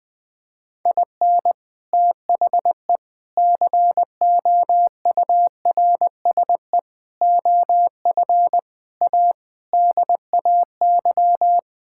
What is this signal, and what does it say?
Morse code, heard as a keyed tone, about 20 wpm, 700 Hz. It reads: IN THE COURSE OF A DAY
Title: Sentences with 4-Word Phrases - 20wpm